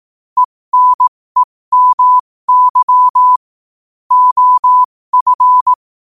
Morse code, heard as a keyed tone, about 18 wpm, spaced as for 15 wpm, 1000 Hz. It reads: ENEMY OF
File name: qotdfiles / 15wpm